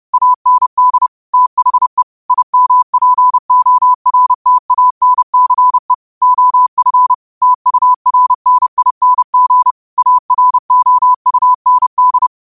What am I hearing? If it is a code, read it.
Morse code, heard as a keyed tone, about 30 wpm, 1000 Hz. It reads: AND THE IMPORTANCE OF TURNING AROUND
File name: qotdfiles / 30wpm